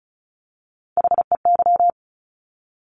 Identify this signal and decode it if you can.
Morse code, heard as a keyed tone, about 36 wpm, 700 Hz. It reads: HEY
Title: Top Three Letter Combinations in Words 35wpm